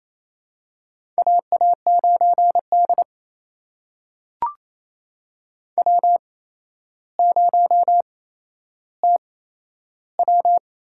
Morse code, heard as a keyed tone, about 28 wpm, spaced as for 12 wpm, 700 Hz. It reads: AA9D W0TW
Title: USA Call Signs - 4 Characters - 8x Character Spacing - 28wpm